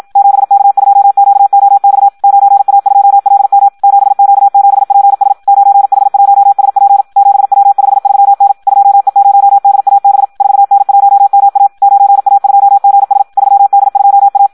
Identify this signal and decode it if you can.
Morse code, heard as a keyed tone, about 56 wpm, 800 Hz. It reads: 7K2QOX 9A2LM 7Q7PS 9H1SP 7W53N 2E0DAB 4N1MA 9A2GS 3D2A
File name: Audio practica CW / qrq-tx-54 2